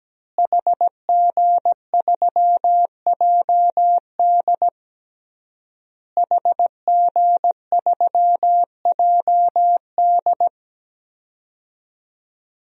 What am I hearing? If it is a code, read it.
Morse code, heard as a keyed tone, about 17 wpm, 700 Hz. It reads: HG3JD HG3JD
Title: DXCC-100 Call Signs - 5 Characters - Repeated 3x - 17wpm